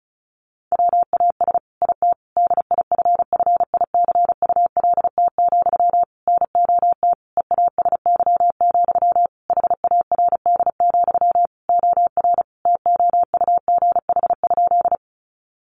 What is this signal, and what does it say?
Morse code, heard as a keyed tone, about 34 wpm, 700 Hz. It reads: WAS IT DIFFICULT, NOT EASY, HARD, OR TOUGH?
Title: Sentences from Top 1200 Words 35wpm